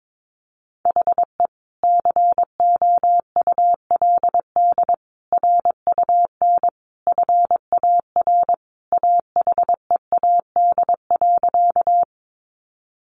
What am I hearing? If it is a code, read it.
Morse code, heard as a keyed tone, about 22 wpm, 700 Hz. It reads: HE COULD RUN FAR AHEAD.
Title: Sentences from Top 1100 Words 22wpm